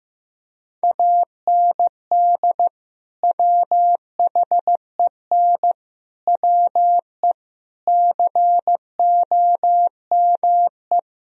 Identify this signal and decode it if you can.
Morse code, heard as a keyed tone, about 15 wpm, 700 Hz. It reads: AND WHEN WE COME
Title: Speed-Racing - Sets of 5 Words 15wpm